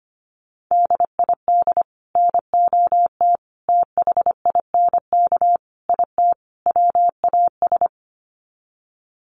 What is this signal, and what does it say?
Morse code, heard as a keyed tone, about 25 wpm, 700 Hz. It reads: DID NOT THINK IT WAS
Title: Sets of 5 Words Encore 25wpm